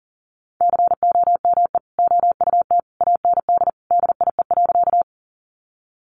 Morse code, heard as a keyed tone, about 40 wpm, 700 Hz. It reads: COME OUT AND DIE.
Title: Sentences from Top 1300 Words 40wpm